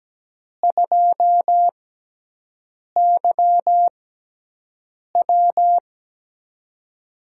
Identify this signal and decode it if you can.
Morse code, heard as a keyed tone, about 17 wpm, spaced as for 6 wpm, 700 Hz. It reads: 2YW
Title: USA Call Signs - 6x Character Spacing - 17wpm